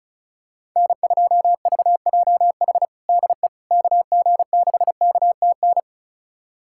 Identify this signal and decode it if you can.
Morse code, heard as a keyed tone, about 36 wpm, 700 Hz. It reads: N2VJH DE KG6KTN
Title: USA Callsign DE Callsign 35wpm